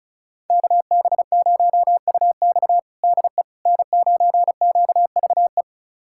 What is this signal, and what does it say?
Morse code, heard as a keyed tone, about 36 wpm, 700 Hz. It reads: KB0UX DE N9QVE
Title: USA Callsign DE Callsign 35wpm